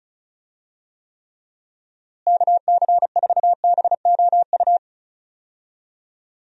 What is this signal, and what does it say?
Morse code, heard as a keyed tone, about 34 wpm, 700 Hz. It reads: KC4BOU
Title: USA Call Signs - 6 Characters - Repeated 2x - 35wpm